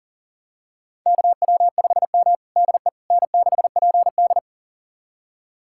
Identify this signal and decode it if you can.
Morse code, heard as a keyed tone, about 37 wpm, 700 Hz. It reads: KW5M DE N6PD
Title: USA Callsign DE Callsign - 4 Characters - 40wpm